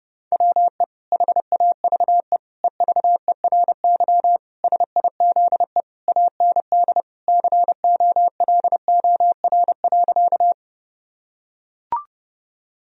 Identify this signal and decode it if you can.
Morse code, heard as a keyed tone, about 32 wpm, 700 Hz. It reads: WE HAVE EVERY SIZE AND COLOR.
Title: Sentences from Top 1000 Words - Review Part 2 - 30wpm